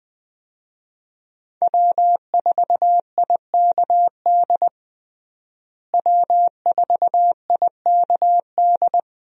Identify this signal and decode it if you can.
Morse code, heard as a keyed tone, about 21 wpm, 700 Hz. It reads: W4IKD W4IKD
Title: USA Call Signs - 5 Characters - Repeated 2x - 20wpm